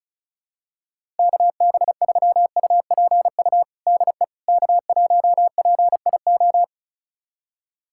Morse code, heard as a keyed tone, about 36 wpm, 700 Hz. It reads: KB3UPU DE K1PIO
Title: USA Callsign DE Callsign 35wpm